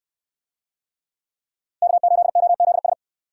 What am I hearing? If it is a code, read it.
Morse code, heard as a keyed tone, about 64 wpm, 700 Hz. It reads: K8OZS